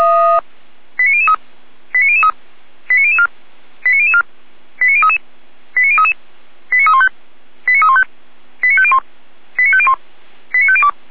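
Five-tone selective call (ZVEI), heard as quick runs of five tones, tones of about 70 ms. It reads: 89003 89003 89004 89004 89033 89033 89315 89315 89501 89501 89502
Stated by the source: ZVEI five-tone selective call received on an SDR, hosted by Feuerwehr Orlamünde.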